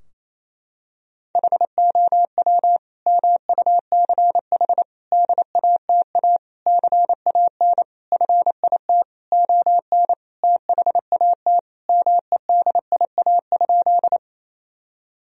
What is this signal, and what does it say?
Morse code, heard as a keyed tone, about 28 wpm, 700 Hz. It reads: HOW MUCH DATA CAN FIT ON THAT MEDIA?
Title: Sentences from Top 1000 Words - Review Part 4 - 28wpm